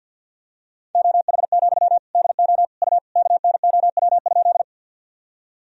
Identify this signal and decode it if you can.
Morse code, heard as a keyed tone, about 51 wpm, 700 Hz. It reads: OH, DO U KNOW?
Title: Sentences from Top 500 Words and Abbreviations 50wpm